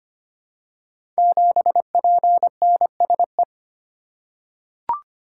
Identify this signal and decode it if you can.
Morse code, heard as a keyed tone, about 25 wpm, 700 Hz. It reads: 7PNSE
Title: Code Groups - 5 Letter:Numbers - 25wpm